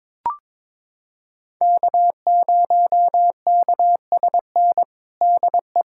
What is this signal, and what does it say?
Morse code, heard as a keyed tone, about 22 wpm, 700 Hz. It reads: K0KSN DE
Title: USA Callsign DE Callsign 22wpm